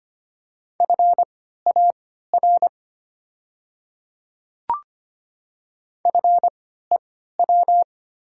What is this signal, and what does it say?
Morse code, heard as a keyed tone, about 25 wpm, spaced as for 14 wpm, 690 Hz. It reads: FAR FEW
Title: 3x Character Spacing - Top 300 Words - 25wpm